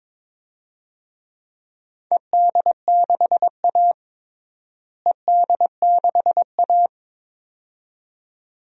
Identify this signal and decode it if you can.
Morse code, heard as a keyed tone, about 21 wpm, 700 Hz. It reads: ED6A ED6A
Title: DXCC-100 Call Signs - 4 Characters - Repeated 2x - 22wpm